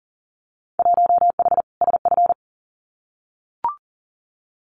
Morse code, heard as a keyed tone, about 41 wpm, 700 Hz. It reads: 1H SF
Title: Winter Field Day Exchanges 40wpm